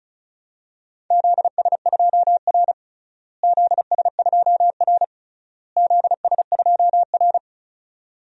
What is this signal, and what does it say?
Morse code, heard as a keyed tone, about 36 wpm, 700 Hz. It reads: ZS2R ZS2R ZS2R
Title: DXCC-100 Call Signs - 4 Characters - Repeated 3x - 35wpm